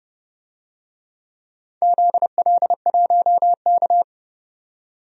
Morse code, heard as a keyed tone, about 30 wpm, 700 Hz. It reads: ZL1K
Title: DXCC-100 Call Signs - 4 Characters - 30wpm